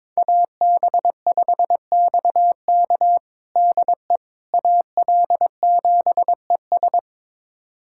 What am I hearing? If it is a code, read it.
Morse code, heard as a keyed tone, about 22 wpm, 700 Hz. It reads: AB5XK DE AL7ES
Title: USA Callsign DE Callsign - 5 Characters - 22wpm